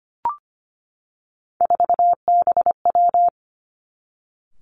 Morse code, heard as a keyed tone, about 25 wpm, 700 Hz. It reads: 4BW
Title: Three Challenging Letter-Number Combinations 25wpm